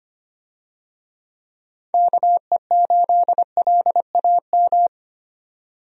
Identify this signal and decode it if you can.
Morse code, heard as a keyed tone, about 25 wpm, 700 Hz. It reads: KE8LAM
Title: USA Call Signs - 6 Characters - 25wpm